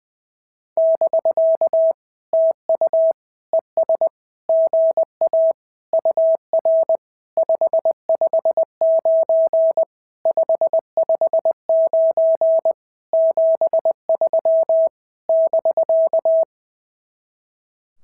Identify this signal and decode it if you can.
Morse code, heard as a keyed tone, about 20 wpm, 650 Hz. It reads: <BK> TU ES GA UR 559 559 73 <BK>
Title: SOTA QSO Practice 20wpm